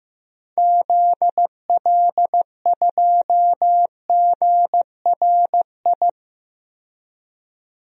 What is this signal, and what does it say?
Morse code, heard as a keyed tone, about 16 wpm, 700 Hz. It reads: ZL2GRI
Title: DXCC-100 Call Signs - 6 Characters - 15wpm